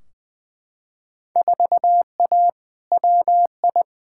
Morse code, heard as a keyed tone, about 20 wpm, 700 Hz. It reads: 4A WI
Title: Field Day Exchanges - 20wpm